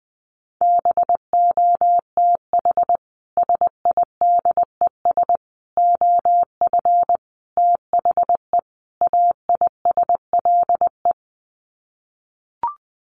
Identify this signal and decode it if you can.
Morse code, heard as a keyed tone, about 20 wpm, 700 Hz. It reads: BOTH SIDES OF THE AISLE
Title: Sets of 5 Words Encore2 20wpm